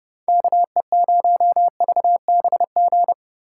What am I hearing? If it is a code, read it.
Morse code, heard as a keyed tone, about 31 wpm, 700 Hz. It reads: KE0VBG
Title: Speed Racing - USA Call Signs 25wpm